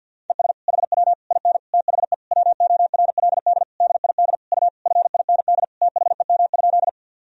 Morse code, heard as a keyed tone, about 46 wpm, 700 Hz. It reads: ES HW IN THE WORLD DID U FIND THEM?
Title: Sentences from Top 200 Words and Abbreviations 50wpm